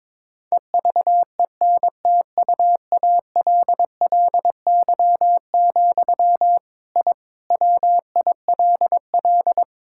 Morse code, heard as a keyed tone, about 21 wpm, 700 Hz. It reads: EVENTUALLY, I WILL
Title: Sentences from Top 1000 Words - Review Part 6 - 22wpm